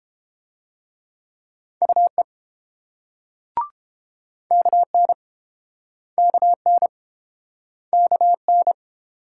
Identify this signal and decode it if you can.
Morse code, heard as a keyed tone, about 31 wpm, 700 Hz. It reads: UE KN KN KN